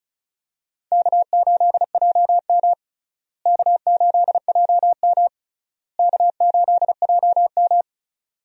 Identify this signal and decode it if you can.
Morse code, heard as a keyed tone, about 36 wpm, 700 Hz. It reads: K8JM K8JM K8JM
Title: USA Call Signs - 4 Characters - Repeated 3x - 35wpm